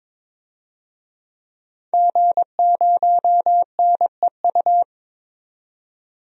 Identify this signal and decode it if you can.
Morse code, heard as a keyed tone, about 21 wpm, 700 Hz. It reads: G0NEU
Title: DXCC-100 Call Signs - 5 Characters - 22wpm